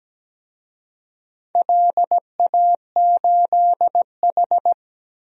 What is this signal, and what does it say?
Morse code, heard as a keyed tone, about 17 wpm, 700 Hz. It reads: LA8H
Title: DXCC-100 Call Signs - 4 Characters - Repeated 2x - 17wpm